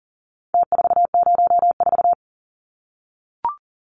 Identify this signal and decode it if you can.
Morse code, heard as a keyed tone, about 40 wpm, 700 Hz. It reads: T404
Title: Numbers - Contest Style - 4-Digit 40wpm